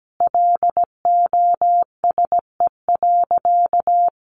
Morse code, heard as a keyed tone, about 17 wpm, 700 Hz. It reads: LOSE.
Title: Sentences from Top 1100 Words 17wpm